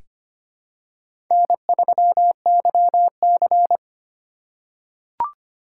Morse code, heard as a keyed tone, about 25 wpm, 690 Hz. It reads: N3YC
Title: USA Call Signs - 4 Characters - 4x Character Spacing - 25wpm